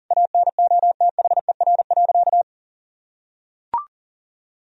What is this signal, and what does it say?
Morse code, heard as a keyed tone, about 40 wpm, 700 Hz. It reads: ANOTHER.
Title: Sentences from Top 1000 Words - Review Part 1 - 40wpm